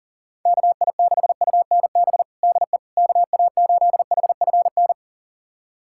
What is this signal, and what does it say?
Morse code, heard as a keyed tone, about 40 wpm, 700 Hz. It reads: KI6UNB DE KA8HFN
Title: USA Callsign DE Callsign - 6 Characters - 40wpm